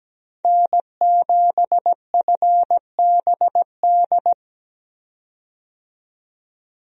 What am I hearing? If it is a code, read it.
Morse code, heard as a keyed tone, about 17 wpm, 700 Hz. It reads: N7FBD
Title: USA Call Signs - 5 Characters - 17wpm